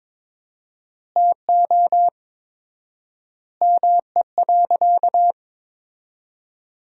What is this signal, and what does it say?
Morse code, heard as a keyed tone, about 22 wpm, 700 Hz. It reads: TO ME.
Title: 4x Word-Spacing - Sentences from Top 400 Words - 22wpm